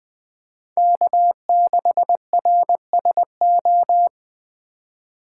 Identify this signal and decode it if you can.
Morse code, heard as a keyed tone, about 24 wpm, 700 Hz. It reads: K6RSO
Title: Speed Racing - USA 1x3 Call Signs 20wpm